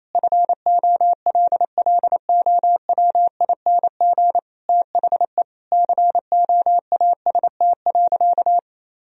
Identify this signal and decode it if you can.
Morse code, heard as a keyed tone, about 28 wpm, 700 Hz. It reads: FOLLOWING THE COAST.
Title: Sentences from Top 1000 Words - Review Part 4 - 28wpm